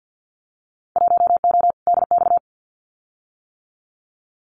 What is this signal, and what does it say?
Morse code, heard as a keyed tone, about 50 wpm, 700 Hz. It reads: 1O DX
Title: Winter Field Day Exchanges 50wpm